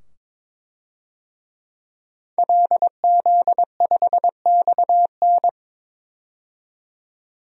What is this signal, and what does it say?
Morse code, heard as a keyed tone, about 22 wpm, 700 Hz. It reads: LZ5XN
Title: DXCC-100 Call Signs - 5 Characters - 22wpm